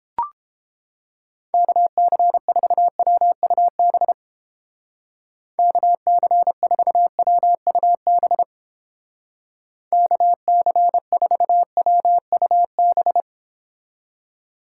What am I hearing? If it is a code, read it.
Morse code, heard as a keyed tone, about 30 wpm, 700 Hz. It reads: KC4WUB KC4WUB KC4WUB